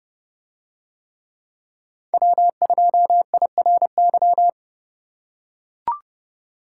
Morse code, heard as a keyed tone, about 28 wpm, 700 Hz. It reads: W2IRY